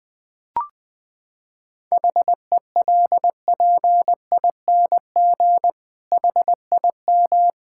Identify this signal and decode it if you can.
Morse code, heard as a keyed tone, about 20 wpm, 700 Hz. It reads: HELPING HIM
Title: Sentences with 7-Word Phrases - 20wpm